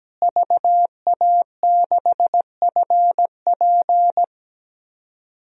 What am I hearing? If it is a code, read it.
Morse code, heard as a keyed tone, about 17 wpm, 700 Hz. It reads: VA6FP
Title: DXCC-100 Call Signs - 5 Characters - Repeated 2x - 17wpm